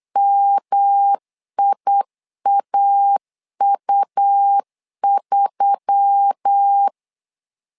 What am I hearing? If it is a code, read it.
Morse code, heard as a keyed tone, about 8 wpm, 800 Hz. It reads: MIAU3
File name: LEZIONI / 4 Q1S3 8wpm 3037